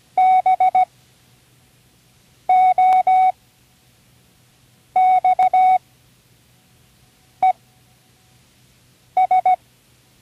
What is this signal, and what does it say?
Morse code, heard as a keyed tone, about 16 wpm, spaced as for 5 wpm, 740 Hz. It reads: BOXES
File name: CWResources / Learning Morse Code (from Tune in the World) / 29 words using all letters